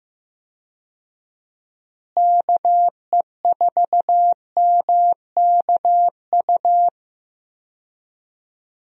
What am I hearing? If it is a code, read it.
Morse code, heard as a keyed tone, about 15 wpm, 700 Hz. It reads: KE4MKU
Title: USA Call Signs - 6 Characters - Repeated 3x - 15wpm